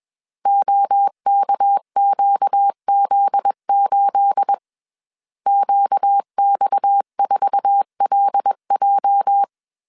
Morse code, heard as a keyed tone, about 21 wpm, 800 Hz. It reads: OKQZ8 QX4LJ
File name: CW / DET / 1 9 1500C S C 20wpm 2310